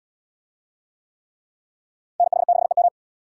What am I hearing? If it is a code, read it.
Morse code, heard as a keyed tone, about 64 wpm, 700 Hz. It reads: NH6EM